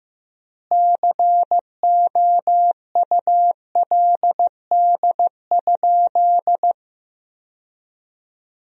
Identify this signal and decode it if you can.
Morse code, heard as a keyed tone, about 15 wpm, 700 Hz. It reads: COULD?
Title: 3x Word-Spacing - Sentences from Top 400 Words - 15wpm